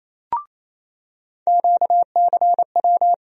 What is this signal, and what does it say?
Morse code, heard as a keyed tone, about 27 wpm, 700 Hz. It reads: QCW
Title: USA Call Sign Postfixes - 3 Letters 28wpm